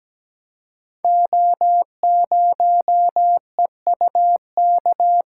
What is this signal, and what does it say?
Morse code, heard as a keyed tone, about 17 wpm, 700 Hz. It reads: O0EUK
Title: Code Groups - 5 Letter:Numbers - 17wpm